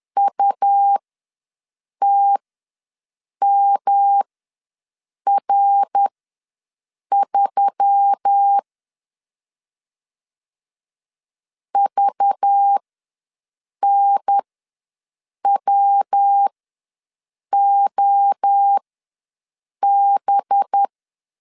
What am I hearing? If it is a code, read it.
Morse code, heard as a keyed tone, about 10 wpm, spaced as for 6 wpm, 800 Hz. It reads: UTMR3 VNWOB
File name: LEZIONI / 5 LBVC / 5 LBVC ws 10wpm 3733